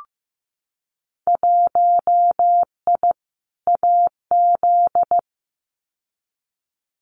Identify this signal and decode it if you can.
Morse code, heard as a keyed tone, about 15 wpm, 700 Hz. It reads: 1I AZ
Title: Winter Field Day Exchanges 15wpm